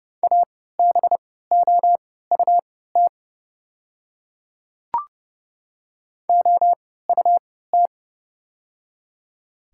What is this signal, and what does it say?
Morse code, heard as a keyed tone, about 30 wpm, spaced as for 17 wpm, 700 Hz. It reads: ABOUT OUT
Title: 3x Character Spacing - Top 100 Words - 30wpm